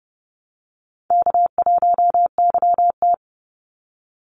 Morse code, heard as a keyed tone, about 30 wpm, 700 Hz. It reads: K1YT
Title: Easy Call Signs Repeated 3x - 30wpm